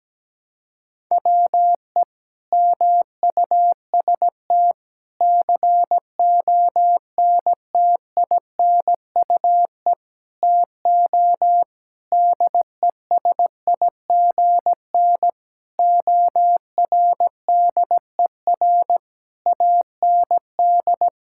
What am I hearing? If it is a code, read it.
Morse code, heard as a keyed tone, about 17 wpm, 700 Hz. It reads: WE MUST CONTINUE TO DESIGN ORDER AND
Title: Sentences from Top 1000 Words - Review Part 6 - 17wpm